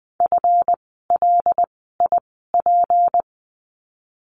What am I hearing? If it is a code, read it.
Morse code, heard as a keyed tone, about 20 wpm, spaced as for 15 wpm, 700 Hz. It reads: FLIP
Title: 153 Lesson 25f - Letter p - Words 20wpm